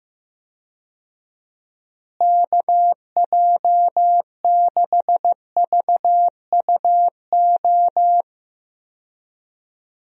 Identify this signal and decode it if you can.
Morse code, heard as a keyed tone, about 15 wpm, 700 Hz. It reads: KJ6VUO
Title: USA Call Signs - 6 Characters - Repeated 3x - 15wpm